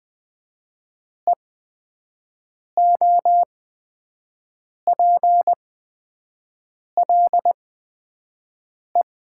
Morse code, heard as a keyed tone, about 20 wpm, spaced as for 5 wpm, 700 Hz. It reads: EOPLE
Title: 8x Character Spacing - Top 200 Words - 20wpm